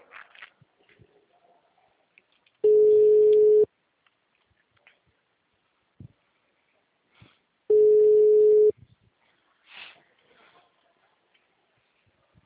Ringback tone (Europe, 425 Hz).
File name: out- -2013- - - .12